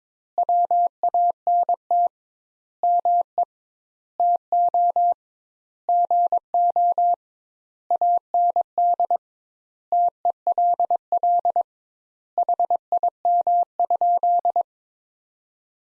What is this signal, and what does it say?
Morse code, heard as a keyed tone, about 22 wpm, 700 Hz. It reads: WANT ME TO GO AND TELL HIM?
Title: Sentences Top 200 Words - 22wpm